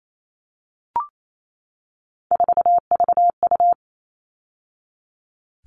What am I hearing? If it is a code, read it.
Morse code, heard as a keyed tone, about 28 wpm, 700 Hz. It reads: 4VU
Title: All Cut Numbers - 3-Digit 28wpm